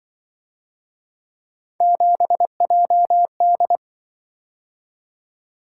Morse code, heard as a keyed tone, about 24 wpm, 700 Hz. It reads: 7JD
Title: Speed-Racing - Three Letter-Number Combinations - 20wpm